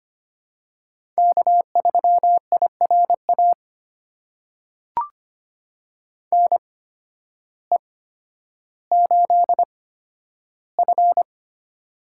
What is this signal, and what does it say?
Morse code, heard as a keyed tone, about 25 wpm, spaced as for 12 wpm, 700 Hz. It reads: K3IRA NE8F
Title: USA Call Signs - 8x Character Spacing - 25wpm